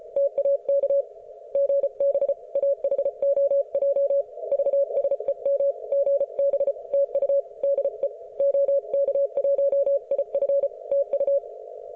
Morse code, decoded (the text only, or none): TAK GB AHOJ VSEM GB TU DE OK1IF TU